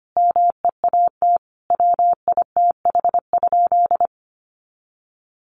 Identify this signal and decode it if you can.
Morse code, heard as a keyed tone, about 25 wpm, 700 Hz. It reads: MEAT WITH?
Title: Sentences from Top 1400 Words 25wpm